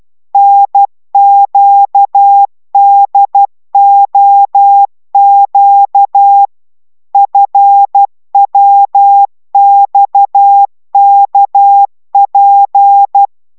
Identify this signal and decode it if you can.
Morse code, heard as a keyed tone, about 12 wpm, 800 Hz. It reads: NQDOQ FWXKP